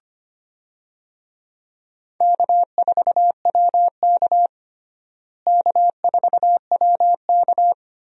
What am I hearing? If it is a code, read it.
Morse code, heard as a keyed tone, about 25 wpm, 700 Hz. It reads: K4WK K4WK
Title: USA Call Signs - 4 Characters - Repeated 2x - 25wpm